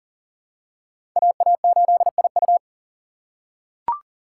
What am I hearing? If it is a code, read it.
Morse code, heard as a keyed tone, about 37 wpm, 700 Hz. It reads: AA8IU
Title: USA Call Signs - 5 Characters - 6x Character Spacing - 40wpm